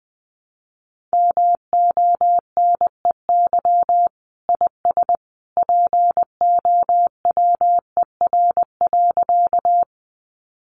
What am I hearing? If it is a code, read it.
Morse code, heard as a keyed tone, about 20 wpm, 700 Hz. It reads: MONEY IS POWER.
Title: Sentences from Top 500 Words 20wpm